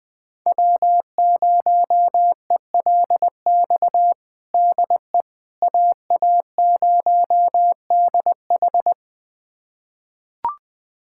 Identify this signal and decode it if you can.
Morse code, heard as a keyed tone, about 20 wpm, 700 Hz. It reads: W0ELX DE AA0DH
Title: USA Callsign DE Callsign - 5 Characters - 20wpm